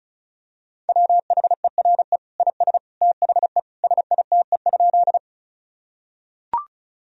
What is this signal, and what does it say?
Morse code, heard as a keyed tone, about 36 wpm, 700 Hz. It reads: WHERE IS THE SITE?
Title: Sentences from Top 1000 Words - Review Part 4 - 35wpm